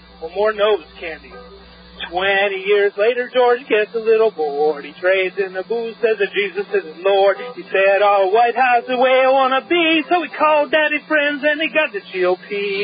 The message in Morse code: none

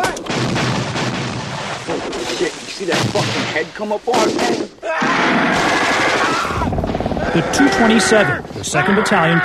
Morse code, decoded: none

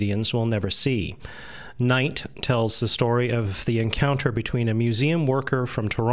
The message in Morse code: none

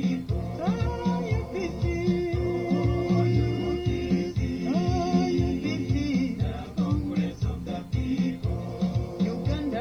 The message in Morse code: none

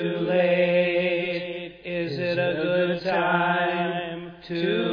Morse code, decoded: none

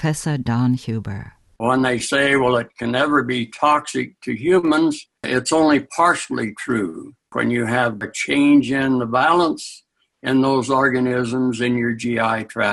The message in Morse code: none

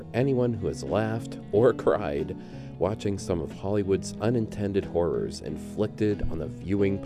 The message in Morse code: none

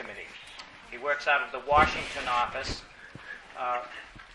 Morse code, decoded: none

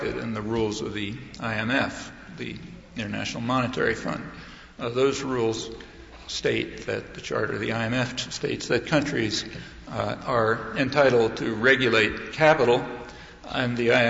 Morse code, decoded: none

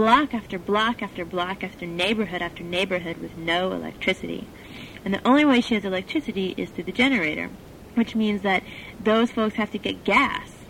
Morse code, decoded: none

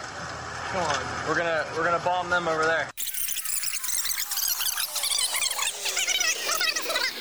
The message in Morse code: none

